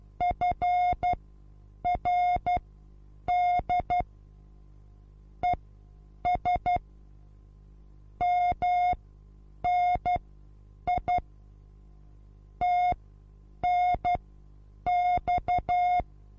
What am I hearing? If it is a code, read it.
Morse code, heard as a keyed tone, about 12 wpm, spaced as for 8 wpm, 720 Hz. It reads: FRD ES MNI TNX
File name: Onlinelehrgang / morsen / DJ4UF Morsen23